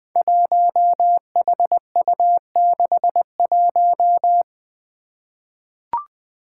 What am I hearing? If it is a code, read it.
Morse code, heard as a keyed tone, about 20 wpm, 700 Hz. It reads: 1HU61
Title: Five Challenging Letter-Number Combinations 20wpm